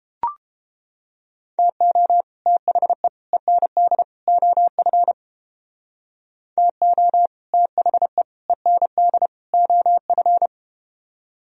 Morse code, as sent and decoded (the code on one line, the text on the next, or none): - --- / - .... / . -. -.. / --- ..-. / - --- / - .... / . -. -.. / --- ..-.
TO THE END OF TO THE END OF